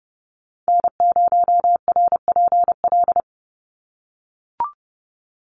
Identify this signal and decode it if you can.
Morse code, heard as a keyed tone, about 30 wpm, 700 Hz. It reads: N0RPL
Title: USA Call Signs 30wpm